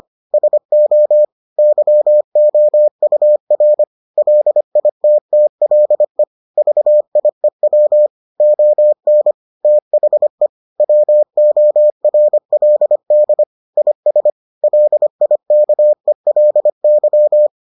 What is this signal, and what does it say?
Morse code, heard as a keyed tone, about 25 wpm, 600 Hz. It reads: SO YOUR LITTLE VIEW ON THE WORLD IS LIKELY